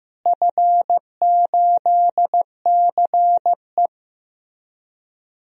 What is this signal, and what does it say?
Morse code, heard as a keyed tone, about 15 wpm, 700 Hz. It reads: F8CE
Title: DXCC-100 Call Signs - 4 Characters - Repeated 3x - 15wpm